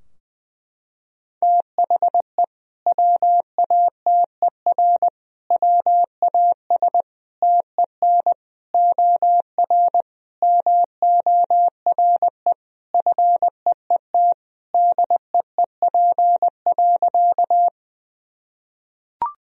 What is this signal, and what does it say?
Morse code, heard as a keyed tone, about 20 wpm, 700 Hz. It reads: THE WATER WAS TEN OR MORE FEET DEEP.
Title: Sentences from Top 1000 Words - Review Part 3 - 20wpm